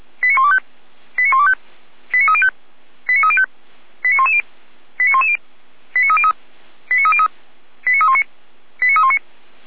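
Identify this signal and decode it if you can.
Five-tone selective call (ZVEI), heard as quick runs of five tones, tones of about 70 ms. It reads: 89125 89125 89305 89305 89110 89110 89303 89303 89319 89319
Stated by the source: ZVEI five-tone selective call received with an SDR.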